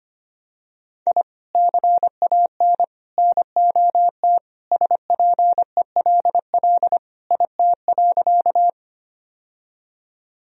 Morse code, heard as a keyed tone, about 25 wpm, 700 Hz. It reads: I CAN NOT SPELL IT.
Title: Sentences from Top 1000 Words - Review Part 1 - 25wpm